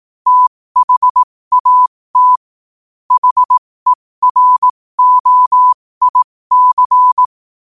Morse code, heard as a keyed tone, about 18 wpm, spaced as for 15 wpm, 1000 Hz. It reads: THAT HEROIC
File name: qotdfiles / 15wpm